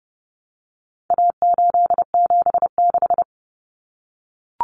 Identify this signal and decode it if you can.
Morse code, heard as a keyed tone, about 30 wpm, 700 Hz. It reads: A876
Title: Common Cut Numbers - 4-Digit 30wpm